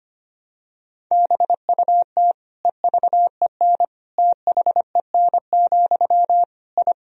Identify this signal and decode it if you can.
Morse code, heard as a keyed tone, about 25 wpm, 700 Hz. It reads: BUT EVEN THEN, I
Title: Sentences from Top 1000 Words - Review Part 1 - 25wpm